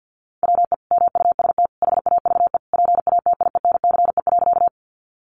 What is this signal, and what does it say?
Morse code, heard as a keyed tone, about 48 wpm, 700 Hz. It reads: WE MUST HAVE PATIENCE.
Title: Sentences from Top 1300 Words 50wpm